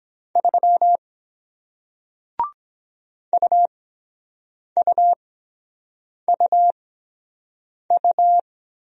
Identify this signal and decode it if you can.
Morse code, heard as a keyed tone, about 23 wpm, spaced as for 7 wpm, 700 Hz. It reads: 3 UUUU